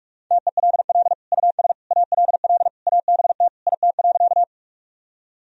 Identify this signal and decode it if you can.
Morse code, heard as a keyed tone, about 46 wpm, 700 Hz. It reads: TELL US ALL ABT IT.
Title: Sentences from Top 200 Words and Abbreviations 45wpm